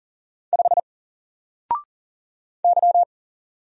Morse code, heard as a keyed tone, about 40 wpm, 700 Hz. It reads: 5 Y
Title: Single Letter-Number Alt 40wpm